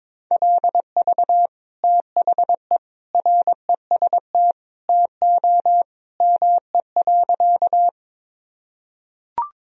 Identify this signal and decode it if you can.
Morse code, heard as a keyed tone, about 22 wpm, 700 Hz. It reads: LV THE REST TO ME.
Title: Sentences from Top 500 Words and Abbreviations 22wpm